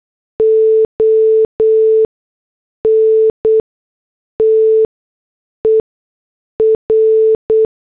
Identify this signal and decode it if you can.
Morse code, heard as a keyed tone, about 8 wpm, spaced as for 6 wpm, 440 Hz. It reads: ONTER
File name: morse / headlines